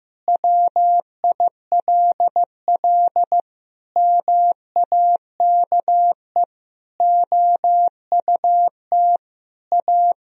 Morse code, heard as a keyed tone, about 16 wpm, 700 Hz. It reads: WILL MAKE OUT A
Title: Sentences from Top 1000 Words - Review Part 2 - 15wpm